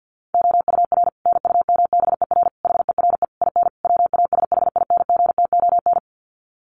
Morse code, heard as a keyed tone, about 48 wpm, 700 Hz. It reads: OUR NUMBER HERE IN WASHINGTON
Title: Sets of 5 Words Encore2 50wpm